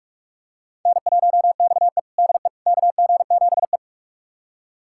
Morse code, heard as a keyed tone, about 46 wpm, 700 Hz. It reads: N1XE DE KG7E
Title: USA Callsign DE Callsign - 4 Characters - 45wpm